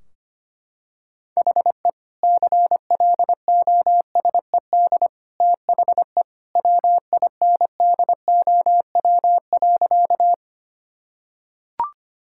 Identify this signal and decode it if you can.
Morse code, heard as a keyed tone, about 25 wpm, 700 Hz. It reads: HE CLOSED THE WINDOW.
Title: Sentences from Top 1600 Words 25wpm